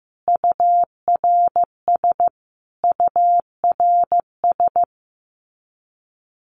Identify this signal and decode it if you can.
Morse code, heard as a keyed tone, about 15 wpm, 700 Hz. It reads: URS URS
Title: Abbreviations Repeated 3x - 15wpm